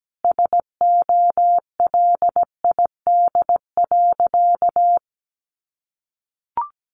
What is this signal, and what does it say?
Morse code, heard as a keyed tone, about 17 wpm, 700 Hz. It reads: SOLID.
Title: Sentences from Top 1500 Words 17wpm